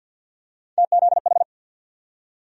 Morse code, heard as a keyed tone, about 53 wpm, 700 Hz. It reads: TZH
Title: USA Call Sign Postfixes - 3 Letters 50wpm